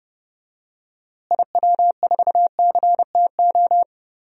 Morse code, heard as a keyed tone, about 30 wpm, 700 Hz. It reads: IW4CTO
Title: DXCC-100 Call Signs - 6 Characters - Repeated 3x - 30wpm